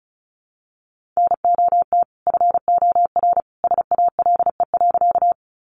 Morse code, heard as a keyed tone, about 34 wpm, 700 Hz. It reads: NOT FOR SALE.
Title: Sentences from Top 1400 Words 35wpm